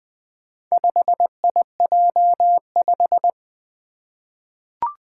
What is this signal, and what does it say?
Morse code, heard as a keyed tone, about 20 wpm, 700 Hz. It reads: 5IJ5
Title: Four Challenging Letter-Number Combinations 20wpm